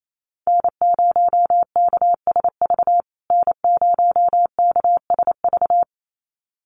Morse code, heard as a keyed tone, about 28 wpm, 700 Hz. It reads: N0KSV N0KSV
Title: USA Call Signs Repeated 2x 28wpm